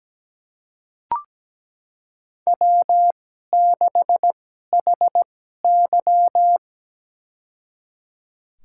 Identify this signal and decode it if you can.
Morse code, heard as a keyed tone, about 17 wpm, spaced as for 12 wpm, 700 Hz. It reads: W6HY
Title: USA Call Signs - 4 Characters - 2x Character Spacing - 17wpm